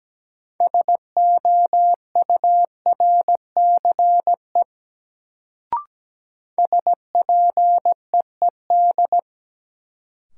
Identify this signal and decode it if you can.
Morse code, heard as a keyed tone, about 17 wpm, 700 Hz. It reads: SOURCE SPEED
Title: Sentences from Top 800 Words - New Word Drill - 17wpm